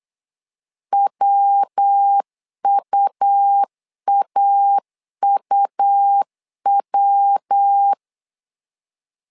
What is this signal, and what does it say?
Morse code, heard as a keyed tone, about 8 wpm, 800 Hz. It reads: WUAUW
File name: LEZIONI / 3 MWRO / 3 MW 8wpm 2501